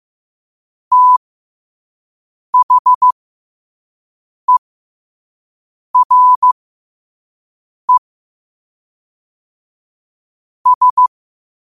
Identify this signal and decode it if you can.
Morse code, heard as a keyed tone, about 15 wpm, spaced as for 5 wpm, 1000 Hz. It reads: THERE S
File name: qotdfiles / 5wpm